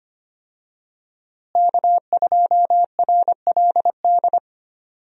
Morse code, heard as a keyed tone, about 25 wpm, 700 Hz. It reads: K2RLD
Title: USA Call Signs - 5 Characters - Repeated 2x - 25wpm